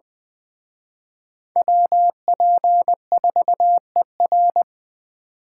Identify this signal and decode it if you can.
Morse code, heard as a keyed tone, about 22 wpm, 700 Hz. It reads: WP4ER